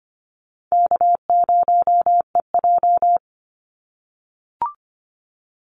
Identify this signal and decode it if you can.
Morse code, heard as a keyed tone, about 25 wpm, 700 Hz. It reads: K0EJ
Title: Easy Call Signs - 25wpm